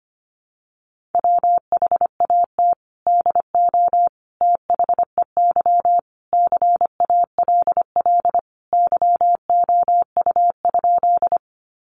Morse code, heard as a keyed tone, about 25 wpm, 700 Hz. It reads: WHAT DO THEY CALL YOU?
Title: Sentences from Top 100 Words 25wpm